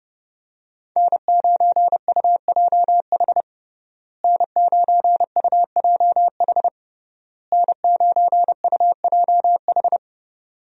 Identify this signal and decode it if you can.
Morse code, heard as a keyed tone, about 30 wpm, 700 Hz. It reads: N9UJH N9UJH N9UJH
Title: USA Call Signs - 5 Characters - Repeated 3x - 30wpm